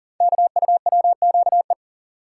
Morse code, heard as a keyed tone, about 42 wpm, 700 Hz. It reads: KUWQE
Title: Code Groups - 5 Letters - 40wpm